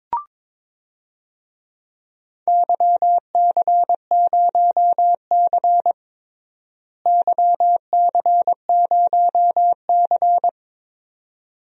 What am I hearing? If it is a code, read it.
Morse code, heard as a keyed tone, about 22 wpm, 700 Hz. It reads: YC0C YC0C
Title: DXCC-100 Call Signs - 4 Characters - Repeated 2x - 22wpm